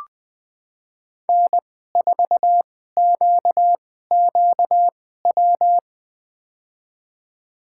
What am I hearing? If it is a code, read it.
Morse code, heard as a keyed tone, about 20 wpm, spaced as for 14 wpm, 700 Hz. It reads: N4QQW
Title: USA Call Signs - 5 Characters - 2x Character Spacing - 20wpm